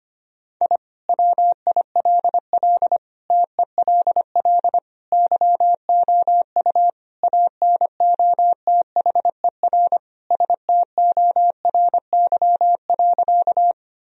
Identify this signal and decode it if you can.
Morse code, heard as a keyed tone, about 25 wpm, 700 Hz. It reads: I WILL TELL YOU ANOTHER STORY.
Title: Sentences from Top 1000 Words - Review Part 1 - 25wpm